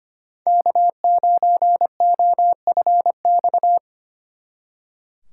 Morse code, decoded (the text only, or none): K9OFX